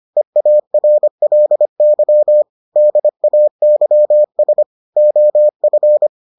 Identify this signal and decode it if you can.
Morse code, heard as a keyed tone, about 25 wpm, 600 Hz. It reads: EARLY DAYS OF